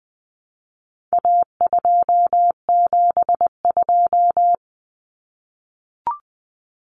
Morse code, heard as a keyed tone, about 20 wpm, 700 Hz. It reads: A272
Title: Common Cut Numbers - 4-Digit 20wpm